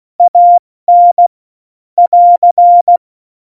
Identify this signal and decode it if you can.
Morse code, heard as a keyed tone, about 16 wpm, 700 Hz. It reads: AN <AR>